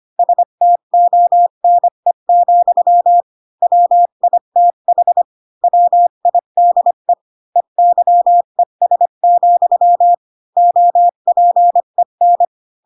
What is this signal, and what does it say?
Morse code, heard as a keyed tone, about 25 wpm, 700 Hz. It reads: STONE, WITH WIDE EYES, OPEN
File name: Pinocchio / Pinocchio ch5 20 25 ik1hge